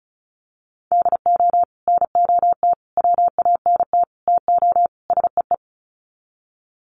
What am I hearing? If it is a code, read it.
Morse code, heard as a keyed tone, about 34 wpm, 700 Hz. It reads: DO NOT WANT TO SEE
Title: Sets of 5 Words 35wpm